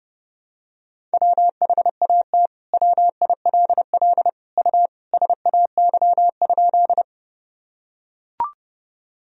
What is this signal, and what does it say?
Morse code, heard as a keyed tone, about 30 wpm, 700 Hz. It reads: WHAT WILL U SAY?
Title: Sentences from Top 200 Words and Abbreviations 30wpm